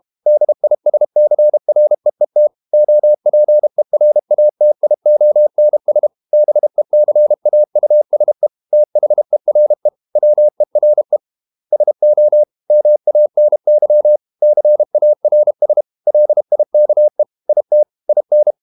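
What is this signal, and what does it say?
Morse code, heard as a keyed tone, about 32 wpm, 600 Hz. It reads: DISCREET OPERATIONS BECAUSE THERE WERE SO MANY CARS LIKE IT IN
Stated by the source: Morse code practice at 32 wpm from CW med Gnister, reading Ken Follett's novel Never in English.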